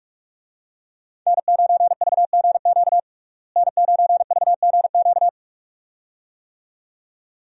Morse code, heard as a keyed tone, about 44 wpm, 700 Hz. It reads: N9VGQ N9VGQ
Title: USA Call Signs - 5 Characters - Repeated 2x - 45wpm